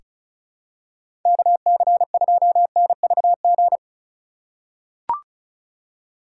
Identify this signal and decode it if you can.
Morse code, heard as a keyed tone, about 36 wpm, 700 Hz. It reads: KC2NVG
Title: USA Call Signs - 4x Character Spacing - 35wpm